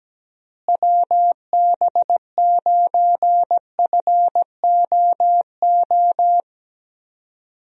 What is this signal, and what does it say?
Morse code, heard as a keyed tone, about 17 wpm, 700 Hz. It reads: WB9FOO
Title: USA Call Signs - 6 Characters - Repeated 3x - 17wpm